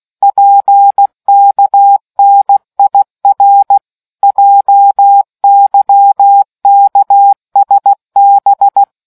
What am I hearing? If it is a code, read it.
Morse code, heard as a keyed tone, about 16 wpm, 800 Hz. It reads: PKNIR JYKSB